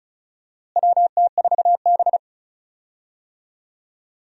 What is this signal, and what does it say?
Morse code, heard as a keyed tone, about 36 wpm, 700 Hz. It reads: WT4B
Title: USA Call Signs - 4 Characters - Repeated 3x - 35wpm